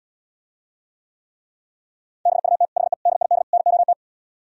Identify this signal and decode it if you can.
Morse code, heard as a keyed tone, about 77 wpm, 700 Hz. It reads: BUT HE DID NOT